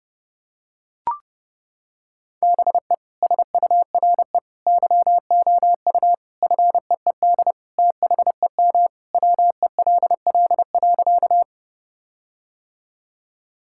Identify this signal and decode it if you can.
Morse code, heard as a keyed tone, about 32 wpm, 700 Hz. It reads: BE SURE YOU FEED THEM WELL.
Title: Sentences from Top 1600 Words 30wpm